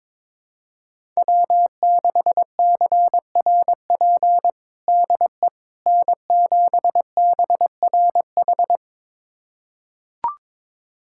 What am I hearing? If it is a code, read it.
Morse code, heard as a keyed tone, about 22 wpm, 700 Hz. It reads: W6CRP DE N7BRH
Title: USA Callsign DE Callsign - 5 Characters - 22wpm